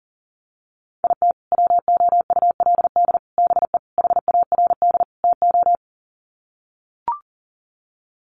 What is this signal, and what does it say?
Morse code, heard as a keyed tone, about 40 wpm, 700 Hz. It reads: IT WOULD BE HARD TO
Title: Sets of 5 Words Encore2 40wpm